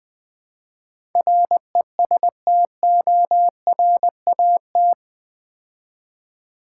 Morse code, heard as a keyed tone, about 20 wpm, 700 Hz. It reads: RESTORAT
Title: Build A Word - 4 Letters At A Time - 11-Letter Words - 20wpm